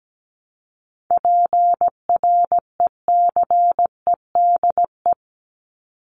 Morse code, heard as a keyed tone, about 17 wpm, 700 Hz. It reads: PRECEDE
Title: ING Suffix - Top 100 Words with Base Words 17wpm